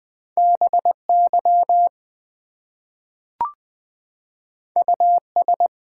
Morse code, heard as a keyed tone, about 20 wpm, 700 Hz. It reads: BY US
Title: Two Letter Words - 20wpm